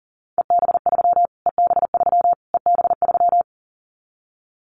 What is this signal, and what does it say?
Morse code, heard as a keyed tone, about 40 wpm, 700 Hz. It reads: EB3 EB3 EB3
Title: Callsign Prefixes for Top 25 DXCC - Repeated 3x - 40wpm